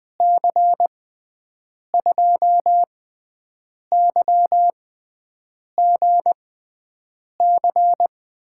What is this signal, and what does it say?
Morse code, heard as a keyed tone, about 20 wpm, spaced as for 7 wpm, 700 Hz. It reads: C2YGC
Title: USA Call Signs - 6 Characters - 6x Character Spacing - 20wpm